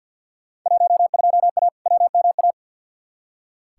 Morse code, heard as a keyed tone, about 46 wpm, 700 Hz. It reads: 12A WMA
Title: Field Day Exchanges Encore - 50wpm